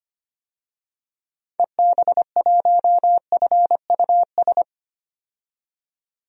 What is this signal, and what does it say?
Morse code, heard as a keyed tone, about 25 wpm, 700 Hz. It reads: EB1FUS
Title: DXCC-100 Call Signs - 6 Characters - 25wpm